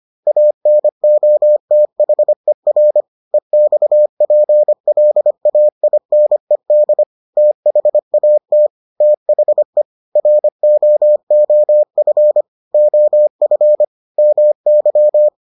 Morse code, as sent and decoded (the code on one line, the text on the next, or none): .- -. --- - .... .-. / . -..- .--. .-.. .- .. -. -.. / - .... .- - / - .... / .-. --- --- ..-. / --- ..-. / -- -.--
ANOTHER EXPLAINED THAT THE ROOF OF MY